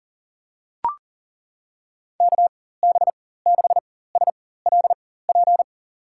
Morse code, heard as a keyed tone, about 40 wpm, spaced as for 19 wpm, 700 Hz. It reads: KB6SLP
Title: USA Call Signs - 6 Characters - 4x Character Spacing - 40wpm